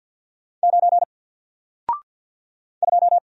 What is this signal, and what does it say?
Morse code, heard as a keyed tone, about 50 wpm, 700 Hz. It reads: 9 2
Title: Single Character (Letter:Number:Punctuation:Prosign) 50wpm